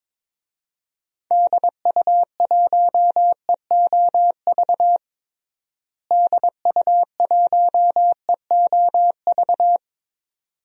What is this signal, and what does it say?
Morse code, heard as a keyed tone, about 22 wpm, 700 Hz. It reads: DU1EOV DU1EOV
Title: DXCC-100 Call Signs - 6 Characters - Repeated 2x - 22wpm